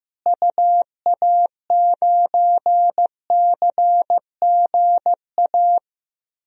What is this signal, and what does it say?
Morse code, heard as a keyed tone, about 16 wpm, 700 Hz. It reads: UA9CGA
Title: DXCC-100 Call Signs - 6 Characters - Repeated 3x - 15wpm